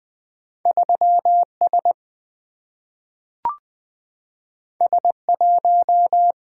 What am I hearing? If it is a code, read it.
Morse code, heard as a keyed tone, about 20 wpm, 700 Hz. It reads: 3 S S 1
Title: Single-Letter Single-Number 20wpm